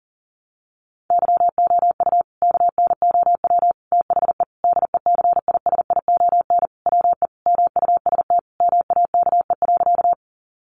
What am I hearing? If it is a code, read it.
Morse code, heard as a keyed tone, about 40 wpm, 700 Hz. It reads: YOU KNOW THE DECISION WE MUST MAKE.
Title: Sentences from Top 1000 Words 40wpm